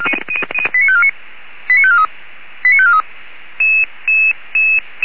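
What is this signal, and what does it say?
Five-tone selective call (ZVEI), heard as quick runs of five tones, tones of about 80 ms. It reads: 89549 89543 89543
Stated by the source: ZVEI five-tone fire brigade alert received on an SDR, Orlamünde.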